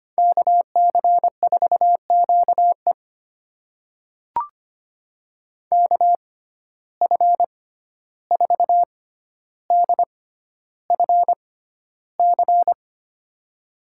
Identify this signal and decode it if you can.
Morse code, heard as a keyed tone, about 25 wpm, spaced as for 12 wpm, 700 Hz. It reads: KC4QE KF4DFC
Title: USA Call Signs - 6x Character Spacing - 25wpm